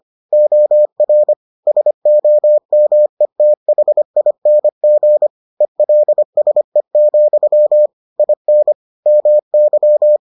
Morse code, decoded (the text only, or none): OR SOMETHING ELSE, IN MY